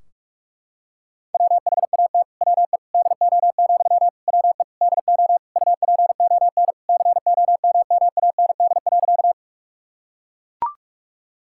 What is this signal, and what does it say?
Morse code, heard as a keyed tone, about 44 wpm, 700 Hz. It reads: WHAT WE DO, WE DO UPON COMMAND.
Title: Sentences from Top 1600 Words 45wpm